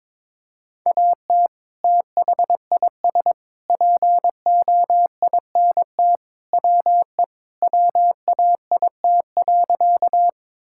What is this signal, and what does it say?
Morse code, heard as a keyed tone, about 22 wpm, 700 Hz. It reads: AT THIS POINT WE WAIT.
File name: practice / 22wpm / Sentences with 3-Word Phrases - 22wpm